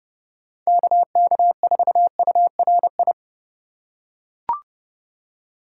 Morse code, heard as a keyed tone, about 30 wpm, 700 Hz. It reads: KK4URI
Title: USA Call Signs - 4x Character Spacing - 30wpm